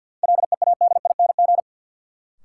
Morse code, heard as a keyed tone, about 52 wpm, 700 Hz. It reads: LEADING
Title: Sentences from Top 1100 Words - New Word Drill Alt - 50wpm